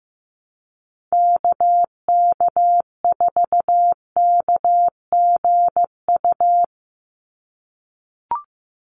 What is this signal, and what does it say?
Morse code, heard as a keyed tone, about 15 wpm, 700 Hz. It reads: KK4KGU
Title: USA Call Signs 15wpm